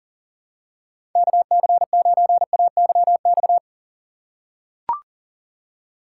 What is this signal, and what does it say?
Morse code, heard as a keyed tone, about 40 wpm, 700 Hz. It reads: KC9AYX
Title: USA Call Signs - 6 Characters - 2x Character Spacing - 40wpm